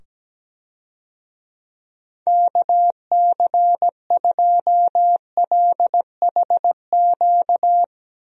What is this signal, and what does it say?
Morse code, heard as a keyed tone, about 17 wpm, 700 Hz. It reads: KC2LHQ
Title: USA Call Signs - 6 Characters - 17wpm